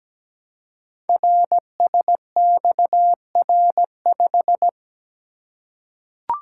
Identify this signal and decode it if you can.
Morse code, heard as a keyed tone, about 17 wpm, 700 Hz. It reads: RSXR5
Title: Code Groups - 5 Letter:Numbers - 17wpm